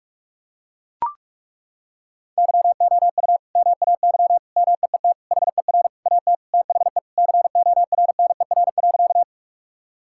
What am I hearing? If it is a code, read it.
Morse code, heard as a keyed tone, about 42 wpm, 700 Hz. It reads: YOU MAY MEET HER AT THE CORNER.
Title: Sentences from Top 1600 Words 45wpm